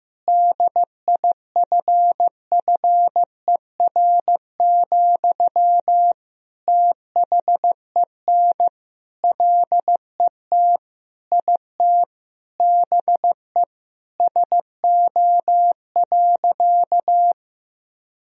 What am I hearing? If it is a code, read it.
Morse code, heard as a keyed tone, about 15 wpm, 700 Hz. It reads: DIFFER, THEN LET IT BE SO.
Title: Sentences from Top 1000 Words - Review Part 1 - 15wpm